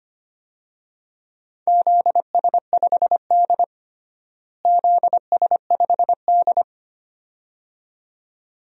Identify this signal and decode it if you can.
Morse code, heard as a keyed tone, about 25 wpm, 700 Hz. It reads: ZS5D ZS5D
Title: DXCC-100 Call Signs - 4 Characters - Repeated 2x - 25wpm